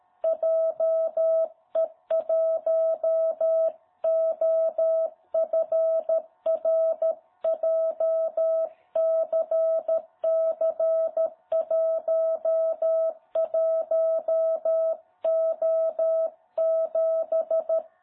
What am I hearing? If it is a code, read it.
Morse code, heard as a keyed tone, about 13 wpm, 660 Hz. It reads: JE1OFR JCC11O7
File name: jh8cbh / 2021cw